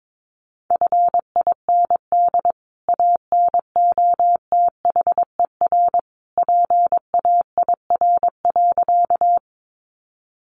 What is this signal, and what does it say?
Morse code, heard as a keyed tone, about 22 wpm, 700 Hz. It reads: FIND ANOTHER PAIR.
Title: Sentences from Top 1200 Words 22wpm